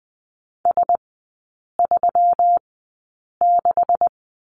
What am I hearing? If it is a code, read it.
Morse code, heard as a keyed tone, about 20 wpm, spaced as for 8 wpm, 700 Hz. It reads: S36
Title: 127 Lesson 21e - Number 6 - Three Characters 20wpm